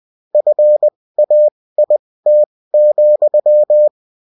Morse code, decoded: FAIT,